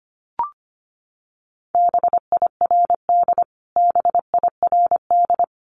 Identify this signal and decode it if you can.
Morse code, heard as a keyed tone, about 25 wpm, 700 Hz. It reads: BIRD BIRD
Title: Top 500 Words Repeated 25wpm